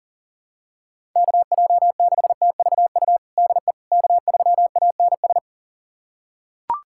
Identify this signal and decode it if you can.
Morse code, heard as a keyed tone, about 40 wpm, 700 Hz. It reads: KJ6TVU DE K3ANS
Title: USA Callsign DE Callsign 40wpm